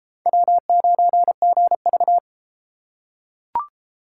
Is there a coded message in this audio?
Yes: Morse code from 0.3 to 2.2 s, heard as a keyed tone, about 32 wpm, 700 Hz. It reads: W9GV